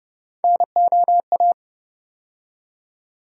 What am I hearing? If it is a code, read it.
Morse code, heard as a keyed tone, about 31 wpm, 700 Hz. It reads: NOA
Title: USA Call Sign Postfixes - 3 Letters 30wpm